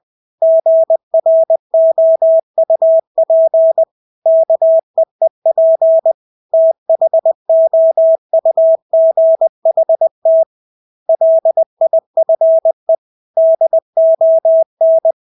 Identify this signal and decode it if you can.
Morse code, heard as a keyed tone, about 20 wpm, 650 Hz. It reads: GROUP KEEP THOUGHT LIFE DON
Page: Engelske ord 20 wpm